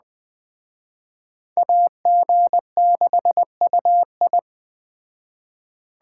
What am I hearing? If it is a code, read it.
Morse code, heard as a keyed tone, about 20 wpm, 700 Hz. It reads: AG6UI
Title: Speed Racing - USA Call Signs 17wpm